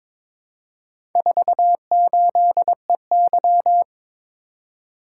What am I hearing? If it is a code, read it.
Morse code, heard as a keyed tone, about 23 wpm, 700 Hz. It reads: 48EY
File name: practice / 22wpm / Four Letter-Number Combinations - 22wpm